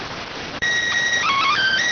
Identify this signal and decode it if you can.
SSTV (slow-scan television), PD 290.